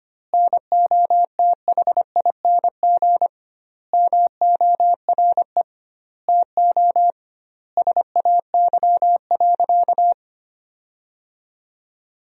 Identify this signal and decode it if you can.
Morse code, heard as a keyed tone, about 25 wpm, 700 Hz. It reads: NOTHING MORE TO SAY.
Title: 2x Word-Spacing - Sentences from Top 500 Words - 25wpm